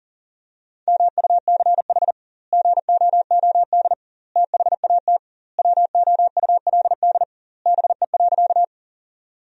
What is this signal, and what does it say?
Morse code, heard as a keyed tone, about 40 wpm, 700 Hz. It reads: MUCH GOOD THAT WOULD BE.
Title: 2x Word-Spacing - Sentences from Top 200 Words - 40wpm